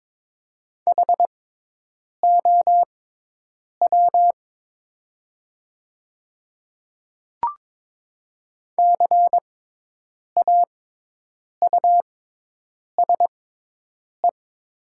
Morse code, heard as a keyed tone, about 21 wpm, spaced as for 8 wpm, 700 Hz. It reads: HOW CAUSE
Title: 6x Character Spacing - Top 200 Words - 22wpm